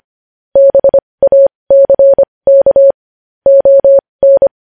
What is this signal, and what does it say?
Morse code, heard as a keyed tone, about 25 wpm, spaced as for 20 wpm, 550 Hz. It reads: BACK ON
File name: podcast / News headlines-1x-25x20